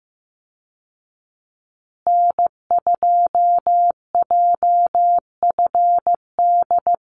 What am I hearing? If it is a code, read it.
Morse code, heard as a keyed tone, about 15 wpm, 700 Hz. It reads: N2JFD
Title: Call Signs Repeated 2x - 15wpm